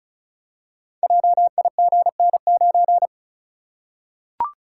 Morse code, heard as a keyed tone, about 36 wpm, 700 Hz. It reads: JIGN9
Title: Code Groups - 5 Letter:Numbers - 35wpm